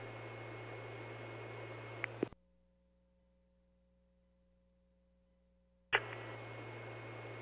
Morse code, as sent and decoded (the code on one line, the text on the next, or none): none